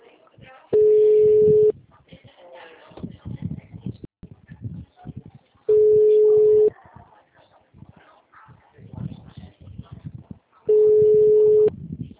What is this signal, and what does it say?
Ringback tone (Europe, 425 Hz), with speech.
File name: out- -2012- - - .3905